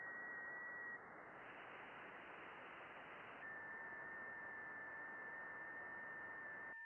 FT8.
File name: contests / june-vhf-k5tr- -0200